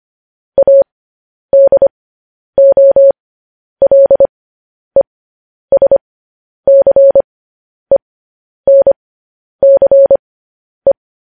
Morse code, heard as a keyed tone, about 25 wpm, spaced as for 10 wpm, 550 Hz. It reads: ADOLESCENCE
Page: news-headlines-in-morse-code-at-10-wpm